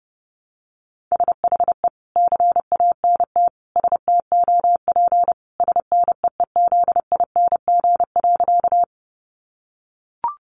Morse code, heard as a keyed tone, about 30 wpm, 700 Hz. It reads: SHE CANT STOP SNEEZING.
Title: Easy Phrases 30wpm